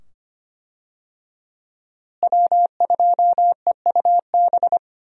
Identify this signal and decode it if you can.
Morse code, heard as a keyed tone, about 25 wpm, 690 Hz. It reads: W2EUB
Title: USA Call Signs - 5 Characters - Repeated 2x - 25wpm